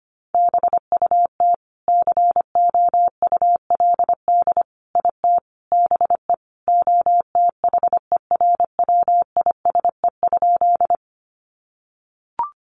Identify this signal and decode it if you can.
Morse code, heard as a keyed tone, about 25 wpm, 700 Hz. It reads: BUT COULD IT BE OTHERWISE?
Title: Sentences from Top 1200 Words 25wpm